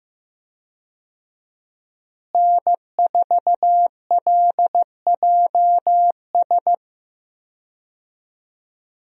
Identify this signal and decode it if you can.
Morse code, heard as a keyed tone, about 15 wpm, 700 Hz. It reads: N4LJS
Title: USA Call Signs - 5 Characters - Repeated 3x - 15wpm